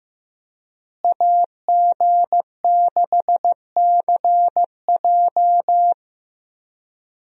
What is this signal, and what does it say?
Morse code, heard as a keyed tone, about 15 wpm, 700 Hz. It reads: AG6CJ
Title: USA Call Signs - 5 Characters - Repeated 3x - 15wpm